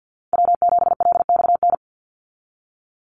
Morse code, heard as a keyed tone, about 48 wpm, 700 Hz. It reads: W7LXN
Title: Call Signs Repeated 3x - 50wpm